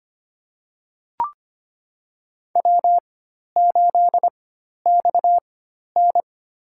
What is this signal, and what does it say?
Morse code, heard as a keyed tone, about 25 wpm, spaced as for 12 wpm, 700 Hz. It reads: W8XN
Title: USA Call Signs - 4 Characters - 4x Character Spacing - 25wpm